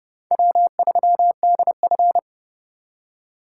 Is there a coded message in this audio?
Yes: Morse code from 0.3 to 2.2 s, heard as a keyed tone, about 30 wpm, 700 Hz. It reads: W3DF